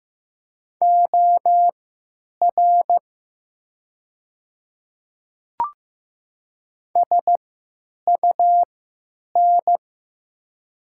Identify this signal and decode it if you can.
Morse code, heard as a keyed tone, about 15 wpm, spaced as for 9 wpm, 700 Hz. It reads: OR SUN
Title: 3x Character Spacing - Top 300 Words - 15wpm